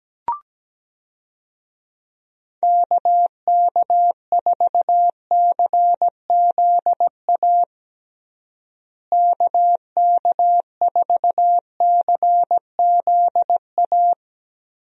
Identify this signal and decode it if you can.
Morse code, heard as a keyed tone, about 17 wpm, 700 Hz. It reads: KK4CZA KK4CZA
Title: USA Call Signs - 6 Characters - Repeated 3x - 17wpm